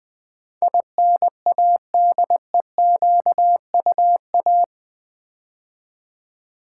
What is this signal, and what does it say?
Morse code, heard as a keyed tone, about 20 wpm, 700 Hz. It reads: INADEQUA
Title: Build A Word - 4 Letters At A Time - 10-Letter Words - 20wpm